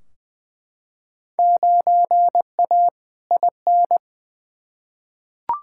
Morse code, heard as a keyed tone, about 20 wpm, 700 Hz. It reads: 9A IN
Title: Field Day Exchanges Encore - 20wpm